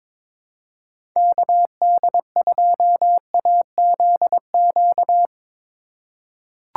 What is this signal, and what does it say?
Morse code, heard as a keyed tone, about 22 wpm, 700 Hz. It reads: KD2AZQ
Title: USA Call Signs - 6 Characters - 2x Character Spacing - 22wpm